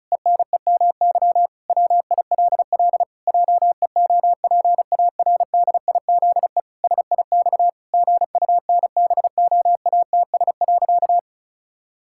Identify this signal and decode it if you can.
Morse code, heard as a keyed tone, about 36 wpm, 700 Hz. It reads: ENEMY WILL JEOPARDIZE SIX GUNBOATS.
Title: Pangram Challenge v2 - 35wpm